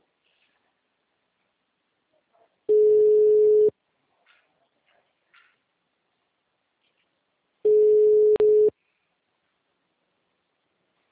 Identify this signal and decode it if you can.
Ringback tone (Europe, 425 Hz).